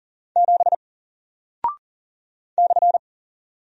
Morse code, heard as a keyed tone, about 40 wpm, 700 Hz. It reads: 7 /
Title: Single Character (Letter:Number:Punctuation:Prosign) Alt 40wpm